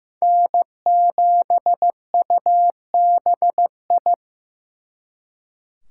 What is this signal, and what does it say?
Morse code, heard as a keyed tone, about 18 wpm, 700 Hz. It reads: N7UBI
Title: Speed Racing - USA 1x3 Call Signs 15wpm